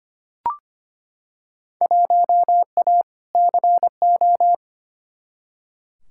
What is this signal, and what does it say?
Morse code, heard as a keyed tone, about 25 wpm, 700 Hz. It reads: 1A CO
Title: Field Day Exchanges - 25wpm